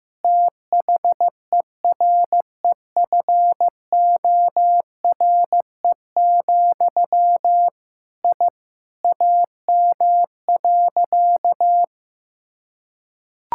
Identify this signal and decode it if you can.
Morse code, heard as a keyed tone, about 15 wpm, 700 Hz. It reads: THEREFORE, I AM.
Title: Sentences from Top 1000 Words - Review Part 5 - 15wpm